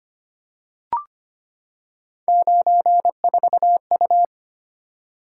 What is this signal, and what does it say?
Morse code, heard as a keyed tone, about 25 wpm, 700 Hz. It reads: 94U
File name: Three Letter-Number Combinations - 25wpm